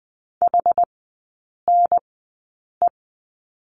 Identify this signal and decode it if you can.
Morse code, heard as a keyed tone, about 20 wpm, spaced as for 8 wpm, 700 Hz. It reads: HNE
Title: 067 Lesson 12e - Letter d - Three Characters 20wpm